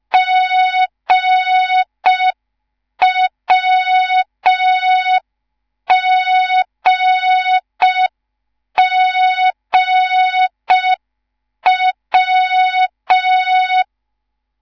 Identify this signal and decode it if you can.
Morse code, heard as a keyed tone, about 5 wpm, 3700 Hz. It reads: GWGGW
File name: Morze-HA2MN / hang / 10GWZ HA2MN